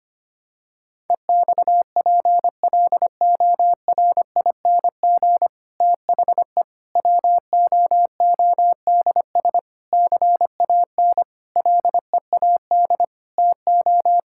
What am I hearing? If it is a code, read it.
Morse code, heard as a keyed tone, about 25 wpm, 700 Hz. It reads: EXPLORING THE WOODS CAN LEAD TO
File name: practice / 25wpm / Difficult Sentences with Hints - 25wpm